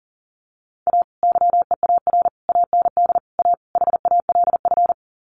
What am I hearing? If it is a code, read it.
Morse code, heard as a keyed tone, about 40 wpm, 700 Hz. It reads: A YEAR AND A HALF
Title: Sets of 5 Words 40wpm